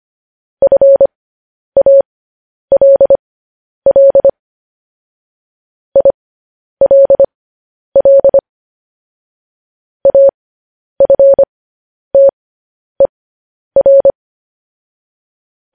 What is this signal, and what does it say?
Morse code, heard as a keyed tone, about 25 wpm, spaced as for 10 wpm, 550 Hz. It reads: FALL ILL AFTER
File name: podcast / News headlines-1x-25x10